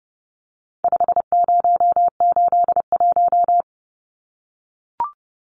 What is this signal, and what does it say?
Morse code, heard as a keyed tone, about 30 wpm, 700 Hz. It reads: 5081
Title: Numbers - 4-Digit 30wpm